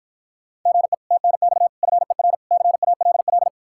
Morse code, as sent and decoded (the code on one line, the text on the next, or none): --. / - -. -..- / ..-. .-. / -.-. .- .-.. .-..
GE TNX FER CALL